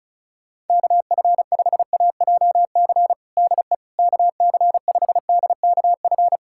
Morse code, heard as a keyed tone, about 36 wpm, 690 Hz. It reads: KF5AJC DE KC5DKF
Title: USA Callsign DE Callsign - 6 Characters - 35wpm